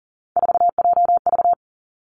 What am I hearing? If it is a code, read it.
Morse code, heard as a keyed tone, about 40 wpm, 700 Hz. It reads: 4JV
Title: Three Challenging Letter-Number Combinations 40wpm